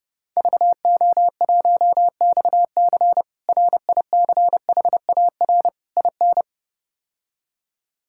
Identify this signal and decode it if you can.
Morse code, heard as a keyed tone, about 32 wpm, 700 Hz. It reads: VO1XC RICHAR IN
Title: K1USN SST QSO Practice - 30wpm